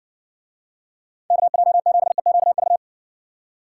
Morse code, heard as a keyed tone, about 58 wpm, 700 Hz. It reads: KJ7EQV